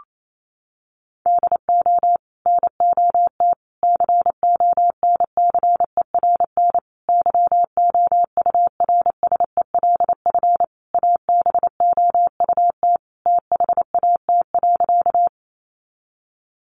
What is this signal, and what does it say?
Morse code, heard as a keyed tone, about 28 wpm, 700 Hz. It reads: DO NOT CONCERN YOURSELF ABOUT THAT.
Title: Sentences from Top 1000 Words 28wpm